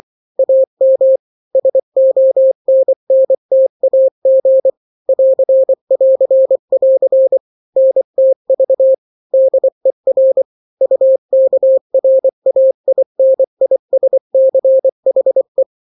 Morse code, heard as a keyed tone, about 24 wpm, 530 Hz. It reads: AM SONNTAG <AR><AR><AR> NTV DER UKRAINISCHE
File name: daily / 23wpm